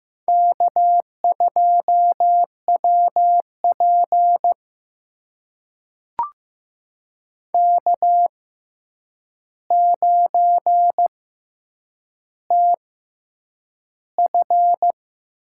Morse code, heard as a keyed tone, about 15 wpm, spaced as for 8 wpm, 700 Hz. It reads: K2WP K9TF
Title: USA Call Signs - 6x Character Spacing - 15wpm